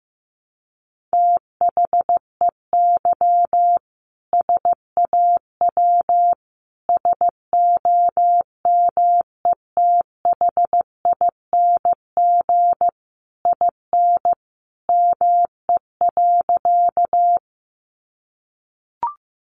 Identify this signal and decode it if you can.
Morse code, heard as a keyed tone, about 15 wpm, 700 Hz. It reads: THEY SAW SOMETHING IN ME.
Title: Sentences from Top 600 Words 15wpm